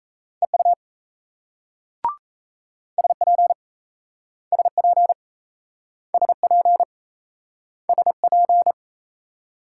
Morse code, heard as a keyed tone, about 38 wpm, 700 Hz. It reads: EU SP SP SP SP